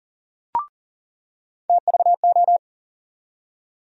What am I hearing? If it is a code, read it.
Morse code, heard as a keyed tone, about 39 wpm, 700 Hz. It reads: TVO
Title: USA Call Sign Postfixes - 3 Letters 40wpm